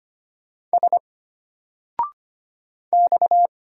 Morse code, heard as a keyed tone, about 25 wpm, 700 Hz. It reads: S X
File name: Single-Letter-Number-25wpm